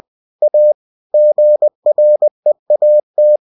Morse code, heard as a keyed tone, about 20 wpm, 600 Hz. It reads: A GREAT
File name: cw-books / winnie-the-pooh / winnie-20wpm-page-0043